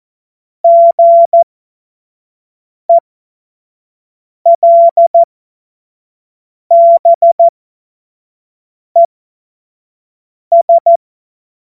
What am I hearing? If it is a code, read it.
Morse code, heard as a keyed tone, about 14 wpm, spaced as for 5 wpm, 680 Hz. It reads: GELBES
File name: Morsekurs / Witze / Witz 19. April 2022 25 F70